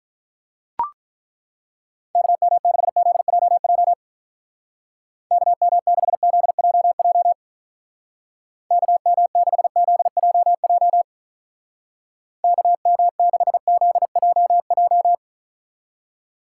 Morse code, heard as a keyed tone, about 43 wpm, 700 Hz. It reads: KM6ZJJ KM6ZJJ KM6ZJJ KM6ZJJ
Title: Speed Racing - USA 2x3 Call Signs 35wpm